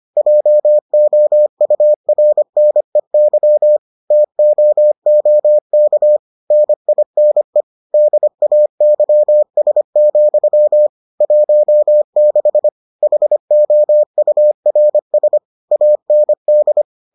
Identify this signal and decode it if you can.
Morse code, heard as a keyed tone, about 25 wpm, 600 Hz. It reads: JOURNEY TOOK NINE DAYS, 16 HOURS AND